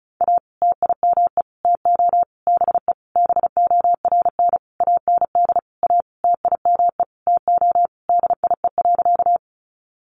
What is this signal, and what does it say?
Morse code, heard as a keyed tone, about 34 wpm, 700 Hz. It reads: A TIME TO BE BORN AND A TIME TO DIE.
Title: Sentences from Top 1300 Words 35wpm